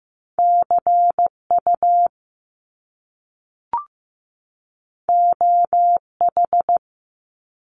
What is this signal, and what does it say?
Morse code, heard as a keyed tone, about 15 wpm, 700 Hz. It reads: CU OH
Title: Top Two Letter Combinations in Words 15wpm